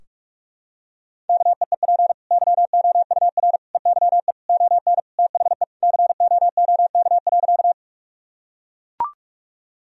Morse code, heard as a keyed tone, about 44 wpm, 700 Hz. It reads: KEEP YOUR EYE ON THE COOK.
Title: Sentences from Top 1000 Words - Review Part 4 - 45wpm